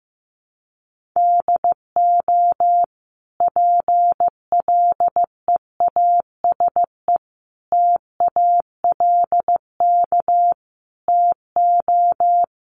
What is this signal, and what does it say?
Morse code, heard as a keyed tone, about 15 wpm, 700 Hz. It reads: DO PLEASE TALK TO
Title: Sentences from Top 1500 Words 15wpm